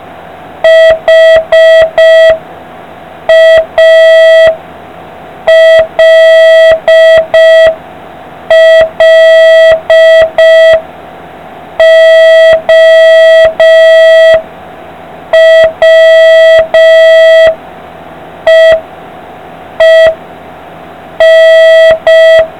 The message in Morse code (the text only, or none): HALLOWEEN